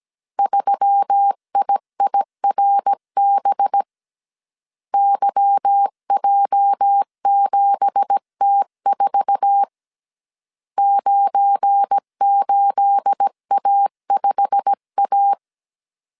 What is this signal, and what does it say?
Morse code, heard as a keyed tone, about 17 wpm, 800 Hz. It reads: 3IIRB YJ7T4 98A5A
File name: 1500C S B 16wpm